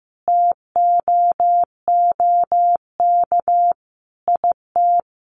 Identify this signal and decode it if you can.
Morse code, heard as a keyed tone, about 15 wpm, 700 Hz. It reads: TOOK IT
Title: Sentences from Top 1400 Words 15wpm